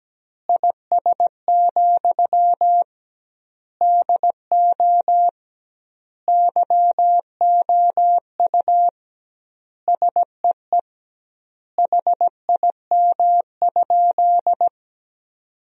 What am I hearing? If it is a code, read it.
Morse code, heard as a keyed tone, about 17 wpm, 700 Hz. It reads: IS, DO YOU SEE HIM?
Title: 2x Word-Spacing - Sentences from Top 100 Words - 17wpm